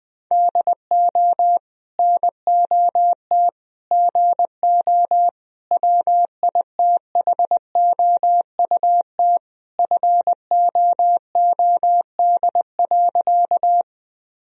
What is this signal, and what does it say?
Morse code, heard as a keyed tone, about 20 wpm, 700 Hz. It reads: DO NOT GO WITHOUT FOOD.
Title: Sentences from Top 1000 Words - Review Part 3 - 20wpm